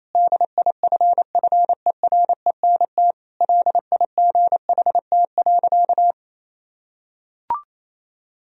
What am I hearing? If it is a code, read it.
Morse code, heard as a keyed tone, about 28 wpm, 700 Hz. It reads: DIFFERENT LIGHT.
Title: Sentences from Top 1000 Words - Review Part 3 - 28wpm